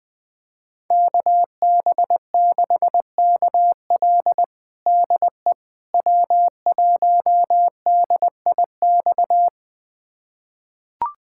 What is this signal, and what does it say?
Morse code, heard as a keyed tone, about 21 wpm, 700 Hz. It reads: KB6KL DE W1DIX
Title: USA Callsign DE Callsign 20wpm